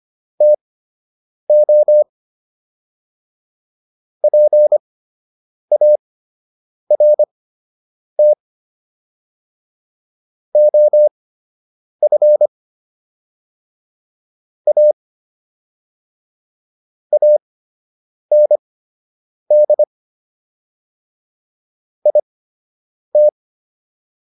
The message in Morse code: TO PART OF A AND IT

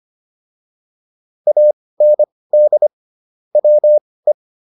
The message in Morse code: AND WE

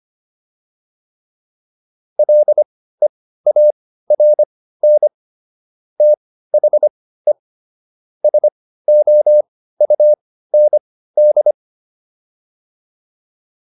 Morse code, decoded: LEARN THE SOUND

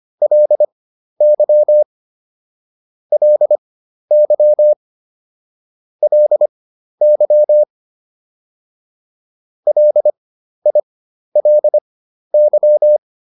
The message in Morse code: LY LY LY LILY

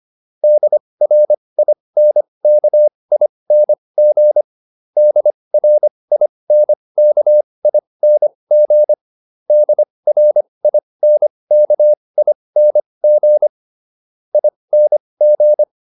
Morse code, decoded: DRINKING DRINKING DRINKING ING

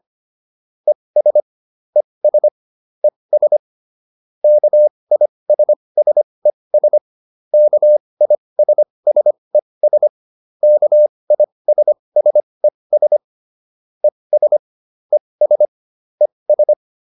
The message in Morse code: ES ES ES KISSES KISSES KISSES ES ES ES

